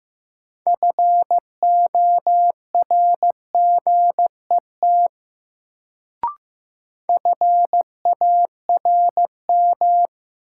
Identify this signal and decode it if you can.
Morse code, heard as a keyed tone, about 15 wpm, 700 Hz. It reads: FORGET FARM
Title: Sentences from Top 1200 Words - New Word Drill Alt - 15wpm